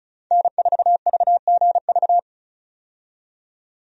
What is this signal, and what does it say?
Morse code, heard as a keyed tone, about 34 wpm, 700 Hz. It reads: N4VGV
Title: USA Call Signs - 5 Characters - 35wpm